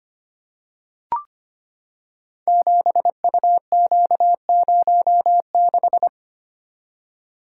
Morse code, heard as a keyed tone, about 25 wpm, 700 Hz. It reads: 7UQ06
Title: Code Groups - 5 Letter:Numbers - 25wpm